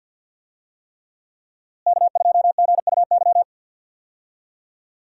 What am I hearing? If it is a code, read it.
Morse code, heard as a keyed tone, about 47 wpm, 700 Hz. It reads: K2GUY
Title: USA Call Signs - 5 Characters - Repeated 3x - 50wpm